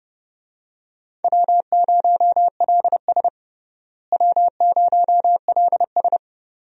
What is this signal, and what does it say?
Morse code, heard as a keyed tone, about 30 wpm, 700 Hz. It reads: W0LS W0LS
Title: USA Call Signs - 4 Characters - Repeated 2x - 30wpm